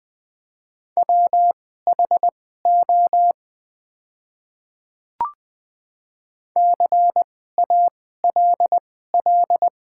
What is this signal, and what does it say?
Morse code, heard as a keyed tone, about 20 wpm, spaced as for 14 wpm, 690 Hz. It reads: WHO CALL
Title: 2x Character Spacing - Top 100 Words - 20wpm